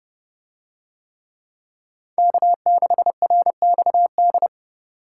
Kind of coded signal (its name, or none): Morse code